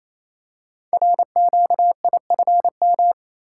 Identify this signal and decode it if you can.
Morse code, heard as a keyed tone, about 27 wpm, 700 Hz. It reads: RQIFM
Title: Code Groups - 5 Letters - 28wpm